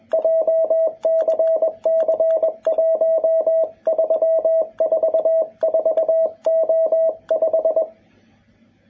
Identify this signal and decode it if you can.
Morse code, heard as a keyed tone, about 21 wpm, 650 Hz. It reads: JCC1344O5
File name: jh8cbh / 2021cw